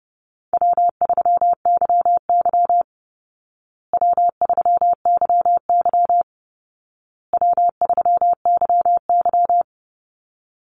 Morse code, heard as a keyed tone, about 30 wpm, 700 Hz. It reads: W3YY W3YY W3YY
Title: Call Signs Repeated 2x - 30wpm